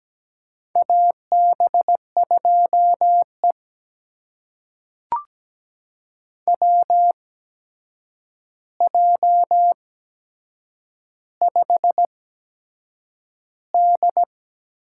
Morse code, heard as a keyed tone, about 17 wpm, spaced as for 7 wpm, 700 Hz. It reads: AB2E WJ5D
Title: USA Call Signs - 4 Characters - 8x Character Spacing - 17wpm